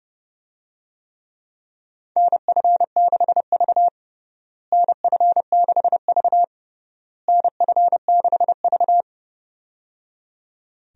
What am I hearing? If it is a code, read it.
Morse code, heard as a keyed tone, about 30 wpm, 700 Hz. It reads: NF6V NF6V NF6V
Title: USA Call Signs - 4 Characters - Repeated 3x - 30wpm